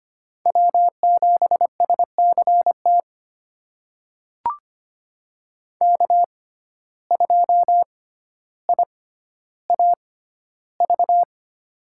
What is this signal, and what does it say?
Morse code, heard as a keyed tone, about 25 wpm, spaced as for 13 wpm, 700 Hz. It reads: W7SCT K2IAV
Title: USA Call Signs - 5 Characters - 6x Character Spacing - 25wpm